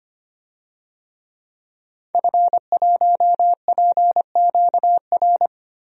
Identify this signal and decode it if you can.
Morse code, heard as a keyed tone, about 25 wpm, 700 Hz. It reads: F1PQR
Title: DXCC-100 Call Signs - 5 Characters - 25wpm